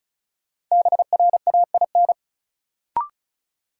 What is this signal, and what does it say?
Morse code, heard as a keyed tone, about 36 wpm, 700 Hz. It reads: BRAIN
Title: Sentences from Top 1000 Words - Review Part 6 - New Word Drill - 35wpm